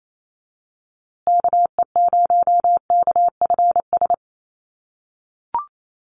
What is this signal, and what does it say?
Morse code, heard as a keyed tone, about 28 wpm, 700 Hz. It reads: KE0KFS
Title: USA Call Signs Repeated 2x 28wpm